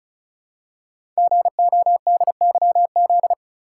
Morse code, heard as a keyed tone, about 36 wpm, 700 Hz. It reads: GODYZ
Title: Code Groups - 5 Letters - 35wpm